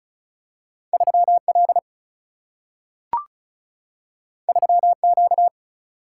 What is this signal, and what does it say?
Morse code, heard as a keyed tone, about 36 wpm, 700 Hz. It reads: 3L 3Q
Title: Single-Letter Single-Number 35wpm